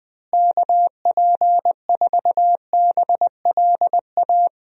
Morse code, heard as a keyed tone, about 21 wpm, 700 Hz. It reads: KP4BLA
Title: DXCC-100 Call Signs - 6 Characters - Repeated 3x - 20wpm